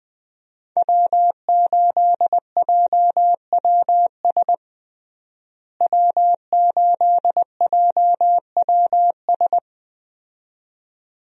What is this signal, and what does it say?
Morse code, heard as a keyed tone, about 20 wpm, 700 Hz. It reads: W8JWS W8JWS
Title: USA Call Signs - 5 Characters - Repeated 2x - 20wpm